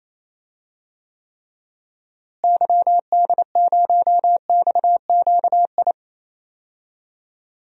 Morse code, heard as a keyed tone, about 28 wpm, 700 Hz. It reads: YD0XQI
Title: DXCC-100 Call Signs - 6 Characters - Repeated 2x - 28wpm